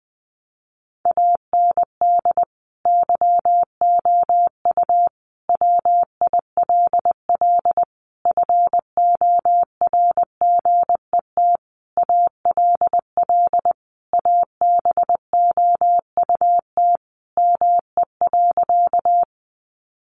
Morse code, heard as a keyed tone, about 20 wpm, 700 Hz. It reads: AND YOU WILL FORGET ALL ABOUT ME.
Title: Sentences from Top 1200 Words 20wpm